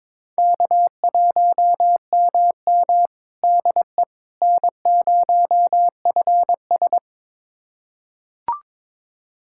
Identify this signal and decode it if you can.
Morse code, heard as a keyed tone, about 22 wpm, 700 Hz. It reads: K1MM DE N0FS
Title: USA Callsign DE Callsign - 4 Characters - 22wpm